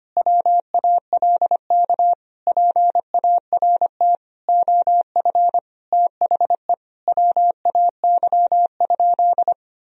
Morse code, heard as a keyed tone, about 25 wpm, 700 Hz. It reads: WALK PART OF THE WAY?
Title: Sentences from Top 1000 Words - Review Part 1 - 25wpm